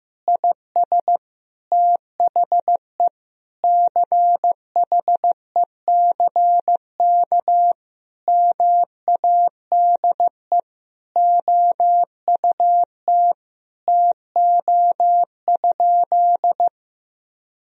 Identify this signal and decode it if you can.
Morse code, heard as a keyed tone, about 16 wpm, 700 Hz. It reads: IS THE CHECK MADE OUT TO?
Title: Sentences from Top 1000 Words - Review Part 3 - 15wpm